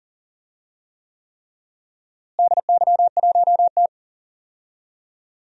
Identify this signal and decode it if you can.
Morse code, heard as a keyed tone, about 40 wpm, 700 Hz. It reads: DY1T
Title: DXCC-100 Call Signs - 4 Characters - Repeated 3x - 40wpm